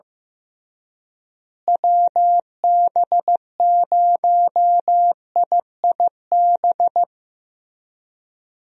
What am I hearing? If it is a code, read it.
Morse code, heard as a keyed tone, about 15 wpm, 700 Hz. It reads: WB0IIB
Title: USA Call Signs - 6 Characters - Repeated 3x - 15wpm